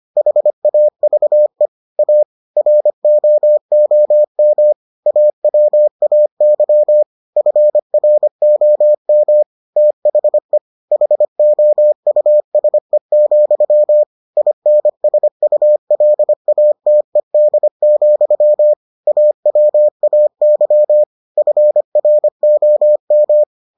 Morse code, heard as a keyed tone, about 25 wpm, 600 Hz. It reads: HAVE A ROOM AWAY FROM THE HOUSE, INSULATED, AWAY FROM